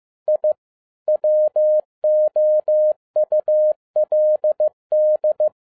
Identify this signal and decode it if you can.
Morse code, heard as a keyed tone, about 15 wpm, 610 Hz. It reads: I WOULD
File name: EXCITING QRP CW OPERATION 15wpm